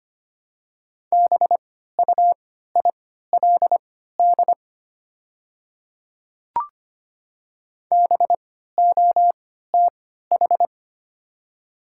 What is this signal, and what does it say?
Morse code, heard as a keyed tone, about 25 wpm, spaced as for 14 wpm, 700 Hz. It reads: BUILD BOTH
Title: 3x Character Spacing - Top 300 Words - 25wpm